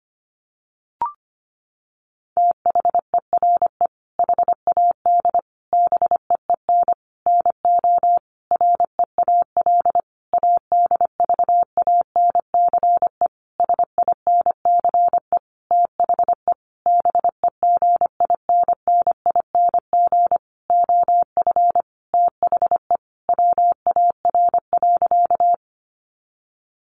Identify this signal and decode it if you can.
Morse code, heard as a keyed tone, about 25 wpm, 700 Hz. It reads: THERE HAD BEEN NO REAL ADVANCE SINCE THE BEGINNING OF THE WAR.
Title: Sentences from Top 1400 Words 25wpm